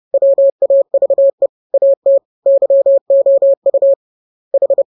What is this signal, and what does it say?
Morse code, heard as a keyed tone, about 30 wpm, 550 Hz. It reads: WAVE AT YOU H
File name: jokes / joke 23 30 0000